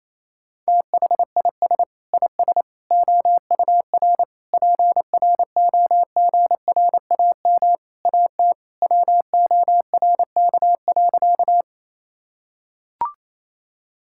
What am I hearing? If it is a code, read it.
Morse code, heard as a keyed tone, about 28 wpm, 700 Hz. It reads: THIS IS OUR PROGRAM AT WORK.
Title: Sentences from Top 1000 Words - Review Part 5 - 28wpm